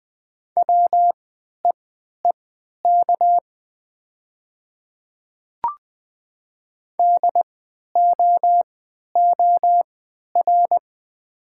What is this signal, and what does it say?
Morse code, heard as a keyed tone, about 20 wpm, spaced as for 11 wpm, 700 Hz. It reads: WEEK DOOR
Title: 3x Character Spacing - Top 500 Words - 20wpm